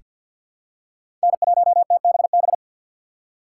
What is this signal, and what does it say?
Morse code, heard as a keyed tone, about 47 wpm, 690 Hz. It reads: N1TBB
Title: USA Call Signs - 5 Characters - 6x Character Spacing - 50wpm